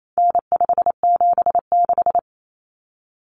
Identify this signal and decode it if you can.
Morse code, heard as a keyed tone, about 28 wpm, 700 Hz. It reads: N576
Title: Common Cut Numbers - 4-Digit 28wpm